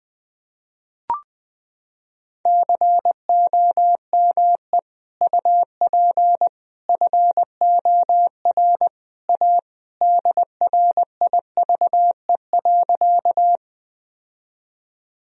Morse code, heard as a keyed tone, about 20 wpm, 700 Hz. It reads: COME UP FOR A DRIVE.
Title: Sentences from Top 1000 Words - Review Part 2 - 20wpm